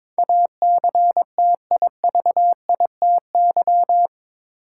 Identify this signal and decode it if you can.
Morse code, heard as a keyed tone, about 22 wpm, 700 Hz. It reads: ACTIVITY